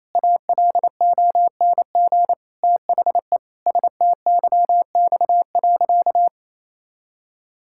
Morse code, heard as a keyed tone, about 28 wpm, 700 Hz. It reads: ALONG THE STYX.
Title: Pangram Challenge v2 - 28wpm